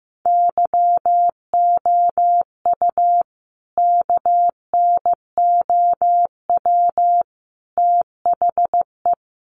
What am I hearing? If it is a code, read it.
Morse code, heard as a keyed tone, about 15 wpm, 700 Hz. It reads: YOU KNOW THE
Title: Sentences from Top 1000 Words 15wpm